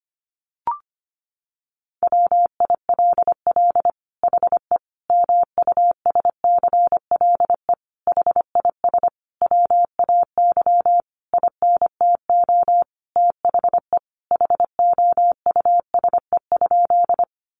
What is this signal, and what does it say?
Morse code, heard as a keyed tone, about 25 wpm, 700 Hz. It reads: WILL HE MUSCLE HIS WAY INTO THE HOUSE?
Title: Sentences from Top 1300 Words 25wpm